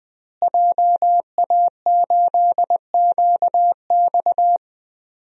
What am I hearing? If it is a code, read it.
Morse code, heard as a keyed tone, about 20 wpm, 700 Hz. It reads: JA8QX
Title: DXCC-100 Call Signs - 5 Characters - Repeated 2x - 20wpm